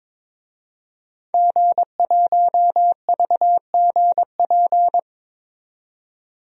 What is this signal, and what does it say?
Morse code, heard as a keyed tone, about 22 wpm, 700 Hz. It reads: G1VGP
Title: DXCC-100 Call Signs - 5 Characters - 22wpm